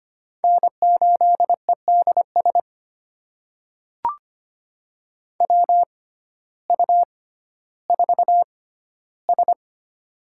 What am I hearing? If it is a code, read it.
Morse code, heard as a keyed tone, about 25 wpm, spaced as for 14 wpm, 700 Hz. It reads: N8EDS WU4S
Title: USA Call Signs - 6x Character Spacing - 25wpm